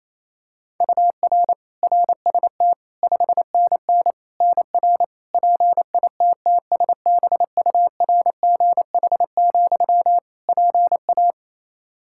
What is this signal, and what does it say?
Morse code, heard as a keyed tone, about 28 wpm, 700 Hz. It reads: UR RST 5NN NR PITTSBURGH, PA
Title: Practice QSO - Exchange 1 - Beginning - 28wpm